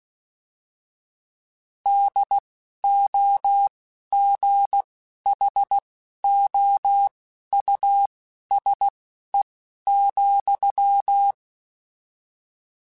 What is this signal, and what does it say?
Morse code, heard as a keyed tone, about 16 wpm, spaced as for 11 wpm, 800 Hz. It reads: DOGHOUSE,